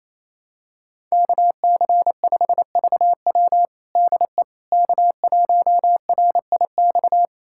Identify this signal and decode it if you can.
Morse code, heard as a keyed tone, about 28 wpm, 700 Hz. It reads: KC5VW DE K1RIX
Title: USA Callsign DE Callsign 28wpm